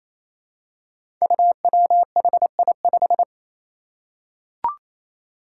Morse code, heard as a keyed tone, about 28 wpm, 700 Hz. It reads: UWHI5
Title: Five Challenging Letter-Number Combinations 28wpm